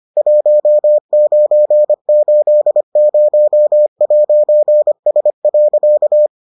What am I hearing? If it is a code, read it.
Morse code, heard as a keyed tone, about 25 wpm, 600 Hz. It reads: 1980'S.